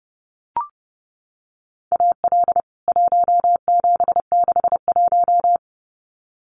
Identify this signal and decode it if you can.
Morse code, heard as a keyed tone, about 30 wpm, 700 Hz. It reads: AL 1761
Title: CWT Exchange 30wpm